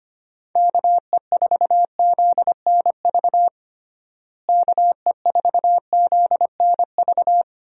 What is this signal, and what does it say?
Morse code, heard as a keyed tone, about 25 wpm, 700 Hz. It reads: KE4ZNV KE4ZNV
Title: USA Call Signs - 6 Characters - Repeated 3x - 25wpm